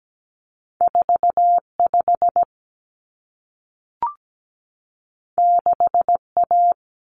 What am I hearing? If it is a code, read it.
Morse code, heard as a keyed tone, about 17 wpm, 700 Hz. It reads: 45 6A
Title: Common Cut Numbers - 2-Digit 17wpm